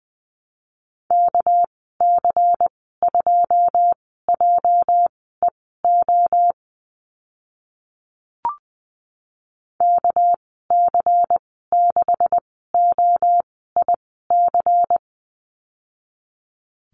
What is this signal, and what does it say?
Morse code, heard as a keyed tone, about 20 wpm, spaced as for 14 wpm, 700 Hz. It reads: KC2JEO KC6OIC
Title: 207 Lesson 33g - Letter j - Call Signs 20wpm